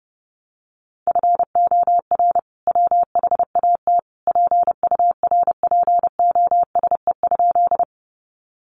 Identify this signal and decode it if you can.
Morse code, heard as a keyed tone, about 30 wpm, 700 Hz. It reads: FOR WHAT PURPOSE?
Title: Sentences from Top 800 Words 30wpm